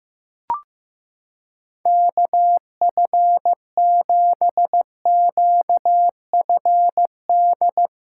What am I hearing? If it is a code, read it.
Morse code, heard as a keyed tone, about 15 wpm, 700 Hz. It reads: KF7QFD
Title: USA Callsign DE Callsign - 6 Characters - 15wpm